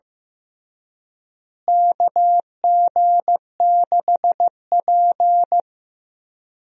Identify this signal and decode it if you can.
Morse code, heard as a keyed tone, about 15 wpm, 700 Hz. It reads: KG6P
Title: USA Call Signs - 4 Characters - Repeated 2x - 15wpm